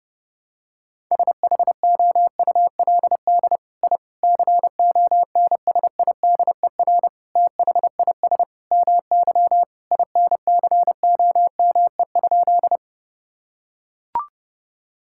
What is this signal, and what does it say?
Morse code, heard as a keyed tone, about 30 wpm, 700 Hz. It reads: SHOULD I CONSIDER THIS MY INCOME?
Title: Sentences from Top 1000 Words - Review Part 5 - 30wpm